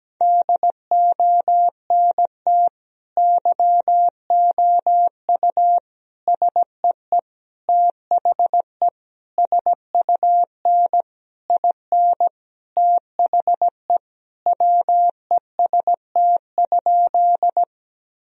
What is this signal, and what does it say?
Morse code, heard as a keyed tone, about 17 wpm, 700 Hz. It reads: DONT YOU SEE THE SUN IN THE WEST?
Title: Sentences from Top 1000 Words - Review Part 2 - 17wpm